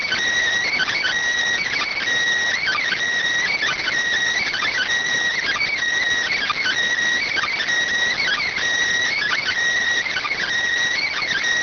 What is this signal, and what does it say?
Emergency Alert System (SAME) tones (header bursts), heard as harsh buzzing bursts. It is heard from 0.6 to 10.4 s.